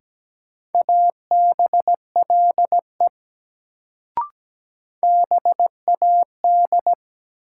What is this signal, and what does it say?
Morse code, heard as a keyed tone, about 17 wpm, 700 Hz. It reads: ABLE BAD
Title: Sentences from Top 700 Words - New Word Drill - 17wpm